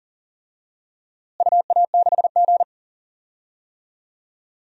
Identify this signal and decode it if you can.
Morse code, heard as a keyed tone, about 40 wpm, 700 Hz. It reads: UA6G
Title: DXCC-100 Call Signs - 4 Characters - Repeated 2x - 40wpm